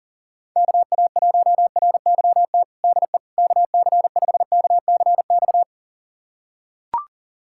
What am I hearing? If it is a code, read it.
Morse code, heard as a keyed tone, about 40 wpm, 700 Hz. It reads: KA1RYT DE KC5KCX
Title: USA Callsign DE Callsign - 6 Characters - 40wpm